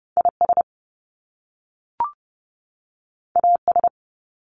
Morse code, heard as a keyed tone, about 30 wpm, 700 Hz. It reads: IS AS
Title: Top Two Letter Combinations in Words 30wpm